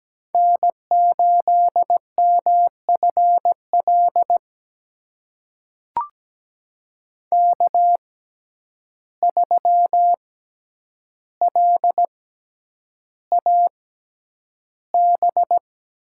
Morse code, heard as a keyed tone, about 17 wpm, spaced as for 9 wpm, 700 Hz. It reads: N8MFL K3LAB
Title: USA Call Signs - 5 Characters - 6x Character Spacing - 17wpm